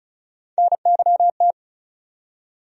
Morse code, heard as a keyed tone, about 34 wpm, 700 Hz. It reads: NYT
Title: USA Call Sign Postfixes - 3 Letters 35wpm